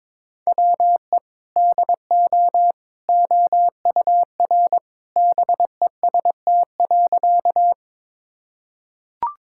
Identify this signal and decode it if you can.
Morse code, heard as a keyed tone, about 22 wpm, 700 Hz. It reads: WE DO OUR BEST.
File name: practice / 22wpm / Sentences with 2-Word Phrases - 22wpm